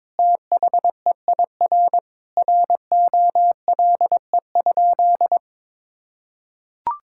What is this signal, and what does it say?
Morse code, heard as a keyed tone, about 22 wpm, 700 Hz. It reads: THEIR ROLE?
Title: Sentences from Top 1000 Words - Review Part 6 - 22wpm